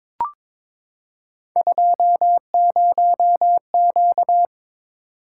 Morse code, heard as a keyed tone, about 22 wpm, 700 Hz. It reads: 20Q
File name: practice / 22wpm / Three Letter-Number Combinations - 22wpm